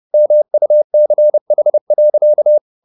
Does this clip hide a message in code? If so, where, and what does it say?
Yes: Morse code from 0.1 to 2.6 s, heard as a keyed tone, about 30 wpm, 600 Hz. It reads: MUCH.